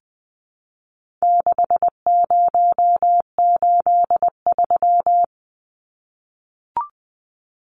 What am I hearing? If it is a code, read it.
Morse code, heard as a keyed tone, about 19 wpm, 700 Hz. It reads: 6083
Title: Numbers - 4-Digit 20wpm